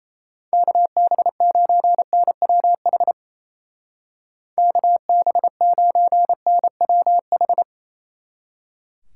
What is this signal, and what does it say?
Morse code, heard as a keyed tone, about 38 wpm, 700 Hz. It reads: KB9NWH KB9NWH